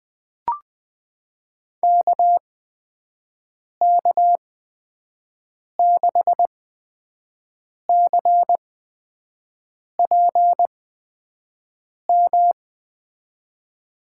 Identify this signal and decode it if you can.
Morse code, heard as a keyed tone, about 21 wpm, spaced as for 5 wpm, 700 Hz. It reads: KK6CPM
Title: USA Call Signs - 6 Characters - 8x Character Spacing - 20wpm